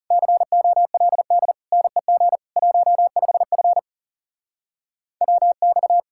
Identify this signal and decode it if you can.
Morse code, heard as a keyed tone, about 39 wpm, 700 Hz. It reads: COLD NEG 15F WX